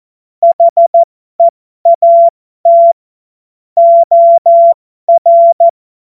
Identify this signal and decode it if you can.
Morse code, heard as a keyed tone, about 14 wpm, spaced as for 12 wpm, 680 Hz. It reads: HEAT OR
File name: Morsekurs / Witze